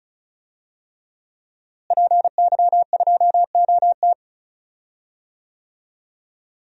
Morse code, heard as a keyed tone, about 36 wpm, 700 Hz. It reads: PY2OT
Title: DXCC-100 Call Signs - 5 Characters - Repeated 3x - 35wpm